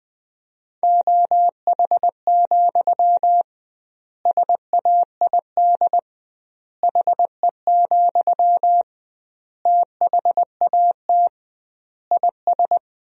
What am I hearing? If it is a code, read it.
Morse code, heard as a keyed tone, about 20 wpm, 700 Hz. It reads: OH, SAID HE, THAT IS
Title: 2x Word-Spacing - Sentences from Top 500 Words - 20wpm